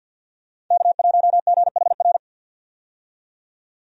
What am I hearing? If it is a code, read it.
Morse code, heard as a keyed tone, about 53 wpm, 700 Hz. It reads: K1GHR
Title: USA Call Signs - 5 Characters - 50wpm